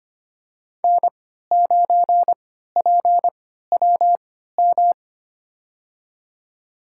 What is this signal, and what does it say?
Morse code, heard as a keyed tone, about 25 wpm, spaced as for 14 wpm, 700 Hz. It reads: N9PWM
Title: USA Call Signs - 5 Characters - 3x Character Spacing - 25wpm